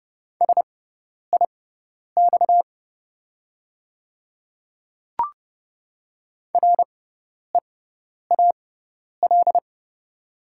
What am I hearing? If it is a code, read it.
Morse code, heard as a keyed tone, about 30 wpm, spaced as for 10 wpm, 690 Hz. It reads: SIX REAL
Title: 6x Character Spacing - Top 500 Words - 30wpm